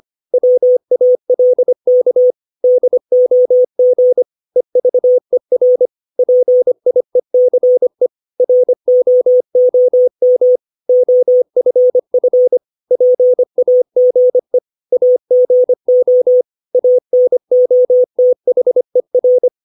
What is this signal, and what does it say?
Morse code, heard as a keyed tone, about 25 wpm, 500 Hz. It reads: WALK DOG EVER PIECE ROOM OFF PAGE AGO ANOTHER
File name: 632 words / 632 Words 25WPM